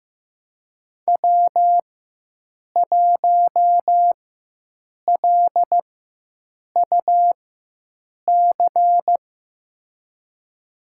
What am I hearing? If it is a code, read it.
Morse code, heard as a keyed tone, about 15 wpm, spaced as for 7 wpm, 700 Hz. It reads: W1LUC
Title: USA Call Signs - 5 Characters - 4x Character Spacing - 15wpm